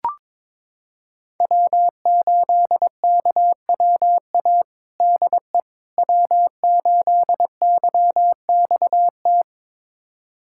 Morse code, heard as a keyed tone, about 21 wpm, 700 Hz. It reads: W8KWA DE W8YXT
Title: USA Callsign DE Callsign - 5 Characters - 22wpm